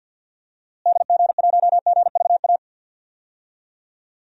Morse code, heard as a keyed tone, about 49 wpm, 700 Hz. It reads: DG1GVA
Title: DXCC-100 Call Signs - 6 Characters - Repeated 2x - 50wpm